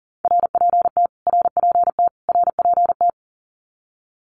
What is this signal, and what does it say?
Morse code, heard as a keyed tone, about 40 wpm, 700 Hz. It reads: RPT RPT RPT
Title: Abbreviations Repeated 3x - 40wpm